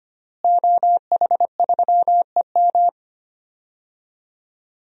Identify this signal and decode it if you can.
Morse code, heard as a keyed tone, about 25 wpm, 700 Hz. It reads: OH3EM
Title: DXCC-100 Call Signs - 5 Characters - Repeated 3x - 25wpm